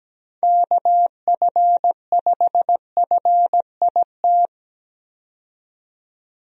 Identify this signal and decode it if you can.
Morse code, heard as a keyed tone, about 17 wpm, 700 Hz. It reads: KF5FIT
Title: USA Call Signs - 6 Characters - 17wpm